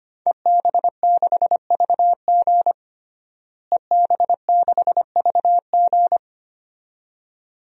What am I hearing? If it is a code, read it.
Morse code, heard as a keyed tone, about 25 wpm, 700 Hz. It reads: EB6VG EB6VG
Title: DXCC-100 Call Signs - 5 Characters - Repeated 2x - 25wpm